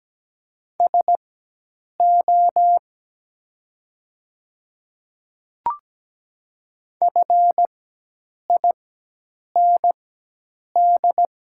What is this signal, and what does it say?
Morse code, heard as a keyed tone, about 17 wpm, spaced as for 8 wpm, 700 Hz. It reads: SO FIND
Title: 4x Character Spacing - Top 100 Words - 17wpm